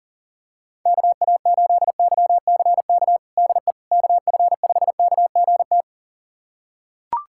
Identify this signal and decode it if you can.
Morse code, heard as a keyed tone, about 40 wpm, 700 Hz. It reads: KA8YCK DE KF5KGT
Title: USA Callsign DE Callsign - 6 Characters - 40wpm